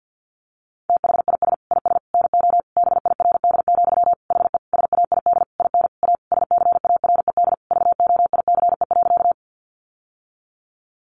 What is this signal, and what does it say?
Morse code, heard as a keyed tone, about 50 wpm, 700 Hz. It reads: THIS IS NO BIRD, HE SAID IN A SCARED VOICE.
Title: Sentences from Top 700 Words 50wpm